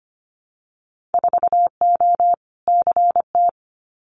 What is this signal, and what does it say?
Morse code, heard as a keyed tone, about 25 wpm, 700 Hz. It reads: 4O CT
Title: Winter Field Day Exchanges Encore 25wpm